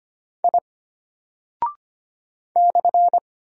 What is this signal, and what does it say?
Morse code, heard as a keyed tone, about 25 wpm, 700 Hz. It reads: I /
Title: Single Character (Letter:Number:Punctuation:Prosign) Alt 25wpm